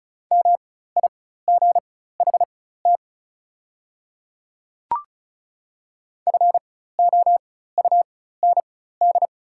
Morse code, heard as a keyed tone, about 34 wpm, spaced as for 16 wpm, 700 Hz. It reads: MIGHT FOUND
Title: 4x Character Spacing - Top 300 Words - 35wpm